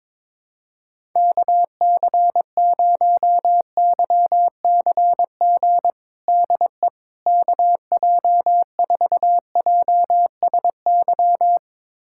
Morse code, heard as a keyed tone, about 21 wpm, 700 Hz. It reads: KC0YCG DE KJ4JSY
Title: USA Callsign DE Callsign - 6 Characters - 22wpm